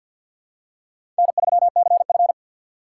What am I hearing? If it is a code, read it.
Morse code, heard as a keyed tone, about 47 wpm, 700 Hz. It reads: N3CF
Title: USA Call Signs - 2x Character Spacing - 50wpm